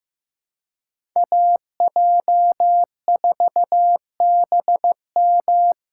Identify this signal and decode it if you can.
Morse code, heard as a keyed tone, about 16 wpm, 700 Hz. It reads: AJ4BM
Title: USA Call Signs - 5 Characters - 2x Character Spacing - 15wpm